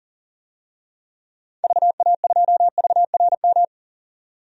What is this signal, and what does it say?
Morse code, heard as a keyed tone, about 40 wpm, 700 Hz. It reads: VA2VRM
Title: DXCC-100 Call Signs - 6 Characters - 40wpm